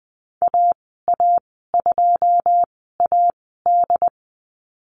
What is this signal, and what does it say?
Morse code, heard as a keyed tone, about 20 wpm, spaced as for 15 wpm, 700 Hz. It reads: AA2AD
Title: 082 Lesson 14g - Number 2 - Call Signs 20wpm